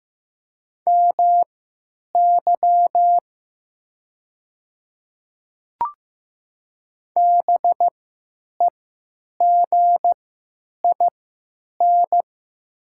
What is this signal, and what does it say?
Morse code, heard as a keyed tone, about 15 wpm, spaced as for 9 wpm, 700 Hz. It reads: MY BEGIN
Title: 3x Character Spacing - Top 300 Words - 15wpm